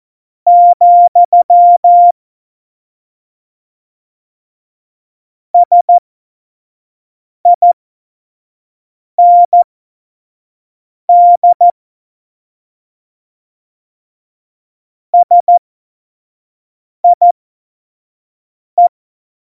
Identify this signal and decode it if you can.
Morse code, heard as a keyed tone, about 14 wpm, spaced as for 5 wpm, 700 Hz. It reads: , SIND SIE
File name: 25 F70